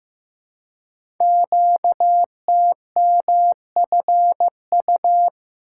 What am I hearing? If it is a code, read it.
Morse code, heard as a keyed tone, about 15 wpm, 700 Hz. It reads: QTMFU
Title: Code Groups - 5 Letters - 15wpm